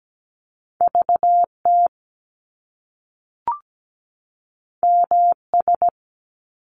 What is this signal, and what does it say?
Morse code, heard as a keyed tone, about 17 wpm, 700 Hz. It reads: VT MS
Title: US States 17wpm